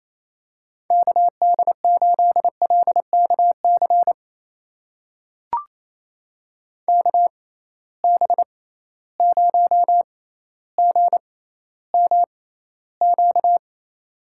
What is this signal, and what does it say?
Morse code, heard as a keyed tone, about 28 wpm, spaced as for 14 wpm, 700 Hz. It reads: KD8LKC KB0GMQ
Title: USA Call Signs - 6 Characters - 6x Character Spacing - 28wpm